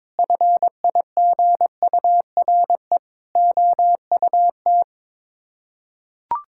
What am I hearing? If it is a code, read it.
Morse code, heard as a keyed tone, about 22 wpm, 700 Hz. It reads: FIGURE OUT
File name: practice / 22wpm / Sets of 8 Words - 22wpm